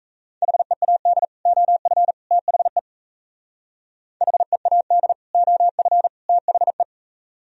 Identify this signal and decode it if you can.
Morse code, heard as a keyed tone, about 37 wpm, 700 Hz. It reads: HEAD OF THE HEAD OF THE